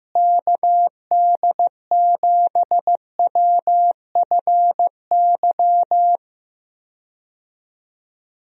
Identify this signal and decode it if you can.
Morse code, heard as a keyed tone, about 15 wpm, 700 Hz. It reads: KD7WFY
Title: USA Call Signs - 6 Characters - Repeated 2x - 15wpm